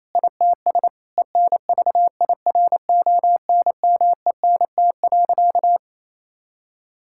Morse code, heard as a keyed tone, about 28 wpm, 700 Hz. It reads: ITS ENVIRONMENT.
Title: Sentences from Top 1000 Words - Review Part 5 - 28wpm